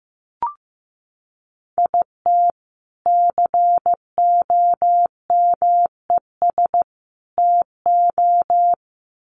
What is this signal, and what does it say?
Morse code, heard as a keyed tone, about 15 wpm, 700 Hz. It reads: IT COMES TO
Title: Sets of 3 Words 15wpm